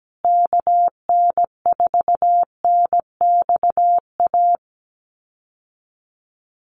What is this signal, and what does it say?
Morse code, heard as a keyed tone, about 17 wpm, 700 Hz. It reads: KN4NXA
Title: USA Call Signs Repeated 3x 17wpm